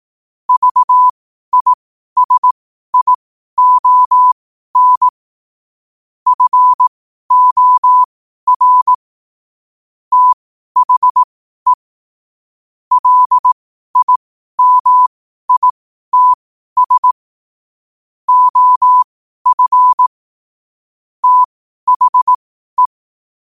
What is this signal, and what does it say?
Morse code, heard as a keyed tone, about 18 wpm, spaced as for 12 wpm, 1000 Hz. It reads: VISION FOR THE LIMITS OF THE
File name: qotdfiles / 12wpm